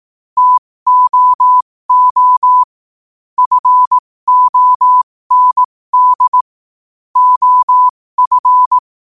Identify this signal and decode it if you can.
Morse code, heard as a keyed tone, about 18 wpm, spaced as for 15 wpm, 1000 Hz. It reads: TOO FOND OF
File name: qotdfiles / 15wpm